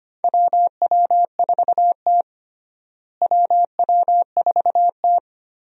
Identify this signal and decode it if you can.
Morse code, heard as a keyed tone, about 25 wpm, 700 Hz. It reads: WW4T WW4T
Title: USA Call Signs - 4 Characters - Repeated 3x - 25wpm